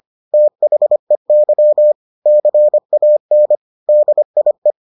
Morse code, heard as a keyed tone, about 25 wpm, 600 Hz. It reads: THEY CAN DIE